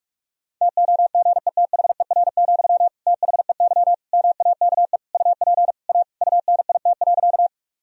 Morse code, heard as a keyed tone, about 46 wpm, 700 Hz. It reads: TOGETHER, THEY MAKE UP A UNIT.
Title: Sentences from Top 1000 Words - Review Part 2 - 45wpm